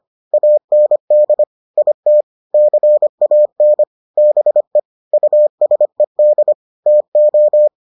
Morse code, heard as a keyed tone, about 25 wpm, 600 Hz. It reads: AND IT CAN BE USED TO